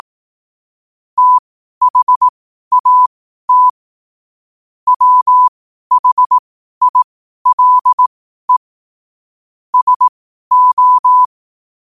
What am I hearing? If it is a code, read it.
Morse code, heard as a keyed tone, about 18 wpm, spaced as for 12 wpm, 1000 Hz. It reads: THAT WHILE SO
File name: qotdfiles / 12wpm